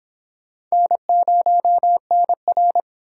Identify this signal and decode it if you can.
Morse code, heard as a keyed tone, about 32 wpm, 700 Hz. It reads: N0NR